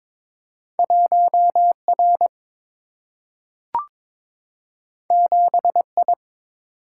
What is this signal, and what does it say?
Morse code, heard as a keyed tone, about 22 wpm, 700 Hz. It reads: 1 R 7 I
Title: Single-Letter Single-Number 22wpm